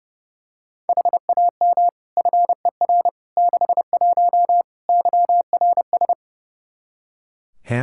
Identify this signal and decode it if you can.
Morse code, heard as a keyed tone, about 28 wpm, 700 Hz. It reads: HAM FER 61 YRS
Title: Practice QSO - Exchange 3 - More Info - 30wpm